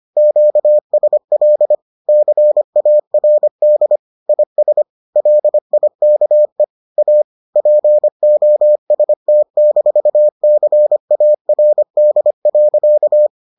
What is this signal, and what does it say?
Morse code, heard as a keyed tone, about 25 wpm, 600 Hz. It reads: QSL CARD IS LIKE A POST-CARD.